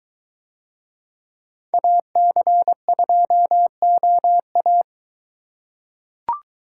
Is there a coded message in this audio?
Yes: Morse code from 1.7 to 4.8 s, heard as a keyed tone, about 22 wpm, 700 Hz. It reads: AC2OA